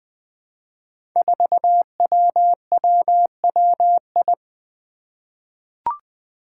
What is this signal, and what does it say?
Morse code, heard as a keyed tone, about 20 wpm, 700 Hz. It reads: 4WWWI
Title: Five Challenging Letter-Number Combinations 20wpm